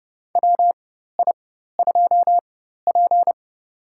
Morse code, heard as a keyed tone, about 30 wpm, spaced as for 14 wpm, 700 Hz. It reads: WI2P
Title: USA Call Signs - 4 Characters - 4x Character Spacing - 30wpm